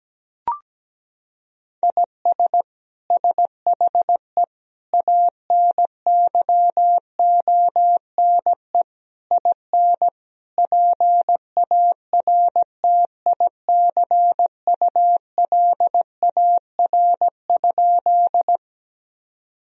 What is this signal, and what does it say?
Morse code, heard as a keyed tone, about 17 wpm, 700 Hz. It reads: IS SHE ANYONE IN PARTICULAR?
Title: Sentences from Top 1000 Words - Review Part 4 - 17wpm